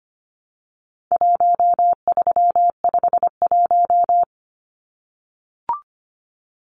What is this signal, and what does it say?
Morse code, heard as a keyed tone, about 25 wpm, 700 Hz. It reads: 1351
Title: Numbers - 4-Digit 25wpm